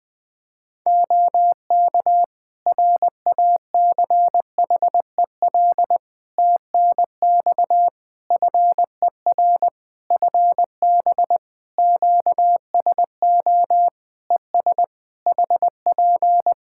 Morse code, heard as a keyed tone, about 21 wpm, 700 Hz. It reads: OK RACHEL TNX FER FB QSO ES HP
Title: K1USN SST QSO Practice 20wpm